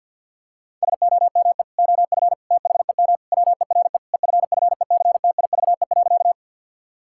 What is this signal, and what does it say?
Morse code, heard as a keyed tone, about 52 wpm, 700 Hz. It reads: SOME OF THEM WERE EFFECTIVE.
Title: Sentences from Top 1000 Words - Review Part 6 - 50wpm